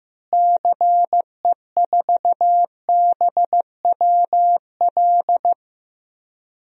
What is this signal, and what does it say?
Morse code, heard as a keyed tone, about 16 wpm, 700 Hz. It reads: CE4BWL
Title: DXCC-100 Call Signs - 6 Characters - Repeated 2x - 15wpm